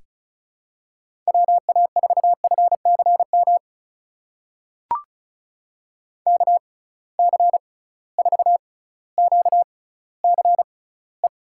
Morse code, heard as a keyed tone, about 34 wpm, spaced as for 18 wpm, 700 Hz. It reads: WA4FCM KC4QCE
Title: USA Call Signs - 6 Characters - 6x Character Spacing - 35wpm